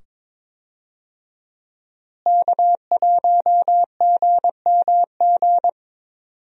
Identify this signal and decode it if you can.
Morse code, heard as a keyed tone, about 22 wpm, 700 Hz. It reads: K1GMG
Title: USA Call Signs - 5 Characters - Repeated 2x - 22wpm